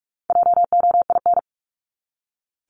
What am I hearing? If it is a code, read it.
Morse code, heard as a keyed tone, about 44 wpm, 700 Hz. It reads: JOIN
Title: Top 1251-1500 Words 45wpm